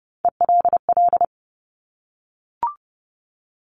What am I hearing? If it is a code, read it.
Morse code, heard as a keyed tone, about 30 wpm, 690 Hz. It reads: ELL
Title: Top Three Letter Combinations in Words 30wpm